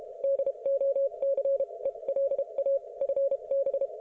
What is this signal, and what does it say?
Morse code, heard as a keyed tone, about 32 wpm, spaced as for 27 wpm, 550 Hz. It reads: DOCELA FB